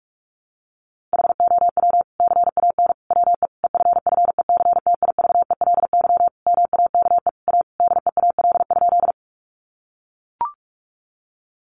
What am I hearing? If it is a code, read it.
Morse code, heard as a keyed tone, about 44 wpm, 700 Hz. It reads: HOW CAN WE EFFECTIVELY MAKE A DEAL?
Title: Sentences from Top 1400 Words 45wpm